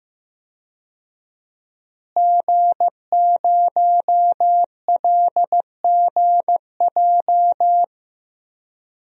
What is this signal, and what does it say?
Morse code, heard as a keyed tone, about 15 wpm, 700 Hz. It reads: G0LGJ
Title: DXCC-100 Call Signs - 5 Characters - Repeated 2x - 15wpm